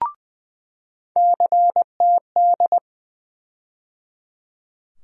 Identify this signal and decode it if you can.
Morse code, heard as a keyed tone, about 20 wpm, 700 Hz. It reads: CTD
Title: USA Call Sign Postfixes - 3 Letters 20wpm